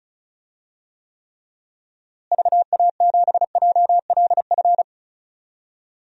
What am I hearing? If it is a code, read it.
Morse code, heard as a keyed tone, about 36 wpm, 700 Hz. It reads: VA7JLF
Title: DXCC-100 Call Signs - 6 Characters - Repeated 2x - 35wpm